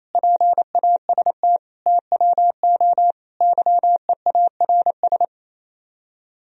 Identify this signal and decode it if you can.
Morse code, heard as a keyed tone, about 32 wpm, 700 Hz. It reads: PAST TWO YEARS
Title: Speed-Racing - Sets of 5 Words 28wpm